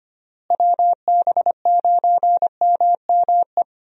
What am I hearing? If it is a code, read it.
Morse code, heard as a keyed tone, about 25 wpm, 700 Hz. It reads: WB9MME
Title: USA Call Signs - 6 Characters - Repeated 3x - 25wpm